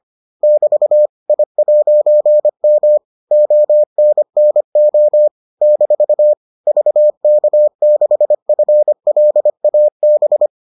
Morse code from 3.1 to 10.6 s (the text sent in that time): ONNO - VK6FLAB